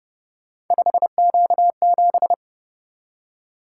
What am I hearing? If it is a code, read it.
Morse code, heard as a keyed tone, about 28 wpm, 700 Hz. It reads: 5Q7
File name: Speed-Racing - Three Letter-Number Combinations - 25wpm